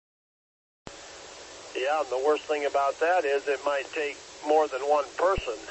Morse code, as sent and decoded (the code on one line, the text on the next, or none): none